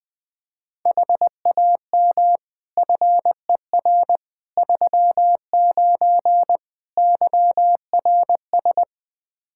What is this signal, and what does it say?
Morse code, heard as a keyed tone, about 20 wpm, 700 Hz. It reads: HAM FER 39 YRS
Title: Practice QSO - Exchange 3 - More Info - 20wpm